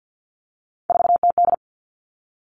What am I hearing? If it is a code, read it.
Morse code, heard as a keyed tone, about 50 wpm, 700 Hz. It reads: 4TD
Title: All Cut Numbers - 3-Digit 50wpm